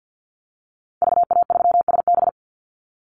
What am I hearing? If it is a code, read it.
Morse code, heard as a keyed tone, about 50 wpm, 700 Hz. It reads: VA3SB
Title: Call Signs Repeated 2x - 50wpm